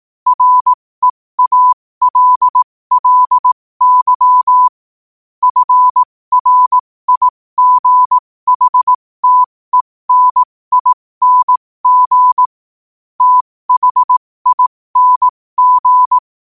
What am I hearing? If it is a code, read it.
Morse code, heard as a keyed tone, about 18 wpm, spaced as for 15 wpm, 1000 Hz. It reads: REALLY FRIGHTENING THING
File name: qotdfiles / 15wpm